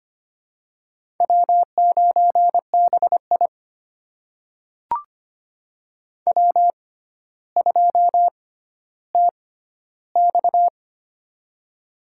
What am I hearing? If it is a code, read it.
Morse code, heard as a keyed tone, about 25 wpm, spaced as for 13 wpm, 700 Hz. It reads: W9BI W2TX
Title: USA Call Signs - 4 Characters - 6x Character Spacing - 25wpm